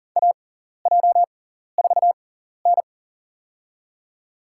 Morse code, heard as a keyed tone, about 37 wpm, spaced as for 13 wpm, 700 Hz. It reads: AJ4N
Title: USA Call Signs - 6x Character Spacing - 40wpm